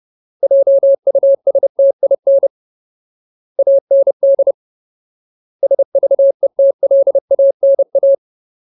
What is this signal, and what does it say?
Morse code, heard as a keyed tone, about 30 wpm, 550 Hz. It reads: JUSTIN AND SVETLANA
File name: Ultimate Word List / 30WPM Ultimate